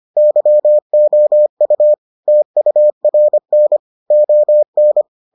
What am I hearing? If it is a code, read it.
Morse code, heard as a keyed tone, about 25 wpm, 600 Hz. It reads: YOU TURN ON